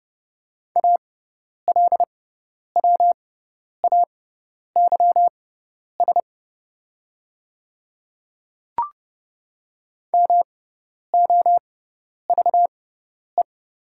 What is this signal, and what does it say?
Morse code, heard as a keyed tone, about 30 wpm, spaced as for 10 wpm, 700 Hz. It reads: ALWAYS MOVE
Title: 6x Character Spacing - Top 300 Words - 30wpm